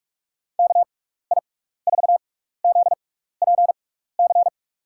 Morse code, heard as a keyed tone, about 44 wpm, spaced as for 15 wpm, 700 Hz. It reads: KI4ZPC
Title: USA Call Signs - 6 Characters - 6x Character Spacing - 45wpm